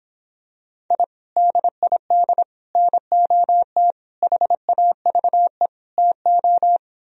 Morse code, heard as a keyed tone, about 26 wpm, 700 Hz. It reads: I DID NOT HAVE TO